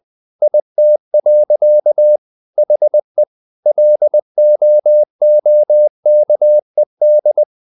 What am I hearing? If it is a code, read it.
Morse code, heard as a keyed tone, about 20 wpm, 600 Hz. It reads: IT. HE LOOKED
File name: the old man and the sea / the old man and the sea part 1 20wpm